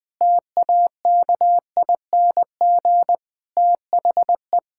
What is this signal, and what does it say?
Morse code, heard as a keyed tone, about 20 wpm, 700 Hz. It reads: TAKING THE
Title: Sentences with 7-Word Phrases - Spoken Beginning - 20wpm